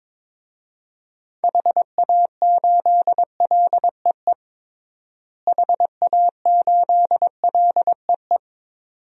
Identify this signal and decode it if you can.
Morse code, heard as a keyed tone, about 21 wpm, 700 Hz. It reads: HA8LEE HA8LEE
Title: DXCC-100 Call Signs - 6 Characters - Repeated 2x - 22wpm